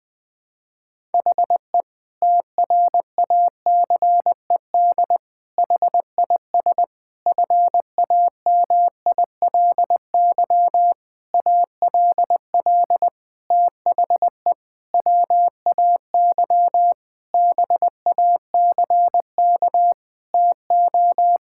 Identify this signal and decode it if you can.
Morse code, heard as a keyed tone, about 20 wpm, 700 Hz. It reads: HE TRACED HIS FAMILY ALL THE WAY BACK TO
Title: Sentences with 6-Word Phrases - 20wpm